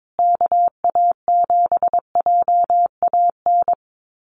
Morse code, heard as a keyed tone, about 22 wpm, 700 Hz. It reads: KA7JAN
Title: USA Call Signs Repeated 2x 22wpm